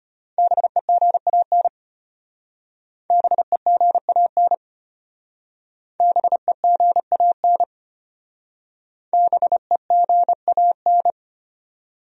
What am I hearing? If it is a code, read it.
Morse code, heard as a keyed tone, about 34 wpm, 700 Hz. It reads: BEGAN BEGAN BEGAN BEGAN